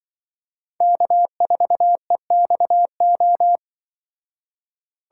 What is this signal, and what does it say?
Morse code, heard as a keyed tone, about 23 wpm, 700 Hz. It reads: K4EXO
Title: Speed Racing - USA Call Signs 20wpm